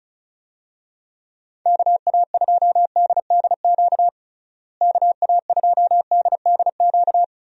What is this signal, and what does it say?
Morse code, heard as a keyed tone, about 36 wpm, 700 Hz. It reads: KA2DDQ KA2DDQ
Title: USA Call Signs - 6 Characters - Repeated 3x - 35wpm